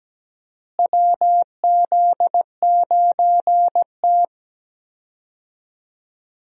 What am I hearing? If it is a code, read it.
Morse code, heard as a keyed tone, about 17 wpm, 700 Hz. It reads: WZ9T
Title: USA Call Signs - 4 Characters - Repeated 2x - 17wpm